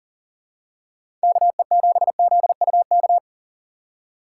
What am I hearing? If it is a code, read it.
Morse code, heard as a keyed tone, about 40 wpm, 700 Hz. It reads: KE7ZUK
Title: USA Call Signs - 6 Characters - 3x Character Spacing - 40wpm